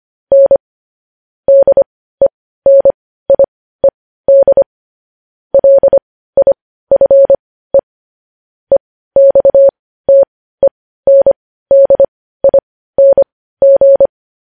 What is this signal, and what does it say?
Morse code, heard as a keyed tone, about 25 wpm, spaced as for 15 wpm, 550 Hz. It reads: N DENIED LIFE EXTENDING
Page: news-headlines-in-morse-code-at-15-wpm